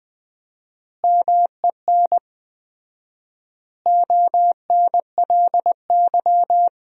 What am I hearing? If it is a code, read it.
Morse code, heard as a keyed tone, about 20 wpm, 700 Hz. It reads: MEN ONLY
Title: 4x Word-Spacing - Sentences from Top 500 Words - 20wpm